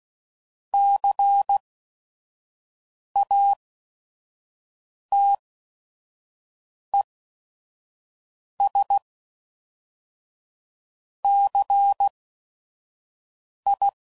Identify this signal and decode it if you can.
Morse code, heard as a keyed tone, about 16 wpm, 800 Hz. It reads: C A T E S C I